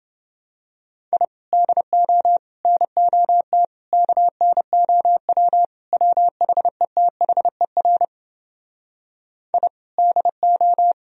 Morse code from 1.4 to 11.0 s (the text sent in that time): DO NOT KNOW WHETHER I DO